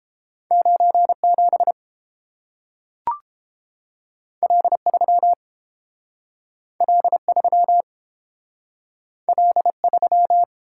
Morse code, heard as a keyed tone, about 30 wpm, 700 Hz. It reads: 97 L3 L3 L3